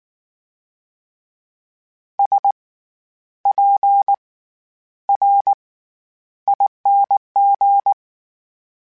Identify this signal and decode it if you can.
Morse code, heard as a keyed tone, about 19 wpm, spaced as for 10 wpm, 800 Hz. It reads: SPRING
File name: Special INGs / Special INGs-mp3s / SlowToFaster ing @18